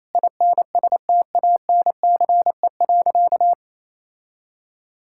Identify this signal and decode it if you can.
Morse code, heard as a keyed tone, about 28 wpm, 700 Hz. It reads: INSTANCE.
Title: Sentences from Top 1000 Words - Review Part 5 - 28wpm